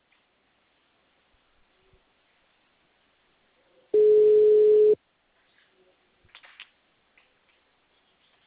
Ringback tone (Europe, 425 Hz).